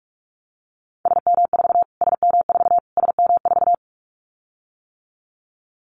Morse code, heard as a keyed tone, about 44 wpm, 700 Hz. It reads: SM4 SM4 SM4
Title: Callsign Prefixes for Top 25 DXCC - Repeated 3x - 45wpm